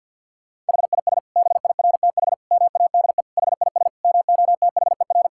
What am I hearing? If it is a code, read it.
Morse code, heard as a keyed tone, about 50 wpm, 700 Hz. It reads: HIS BIRTH MADE HIS MOTHER
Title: Sentences from Top 1000 Words - Review Part 5 - 50wpm